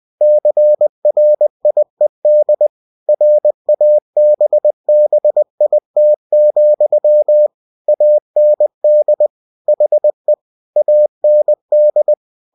Morse code, heard as a keyed tone, about 20 wpm, 600 Hz. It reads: CRIED RABBIT, AND HE AND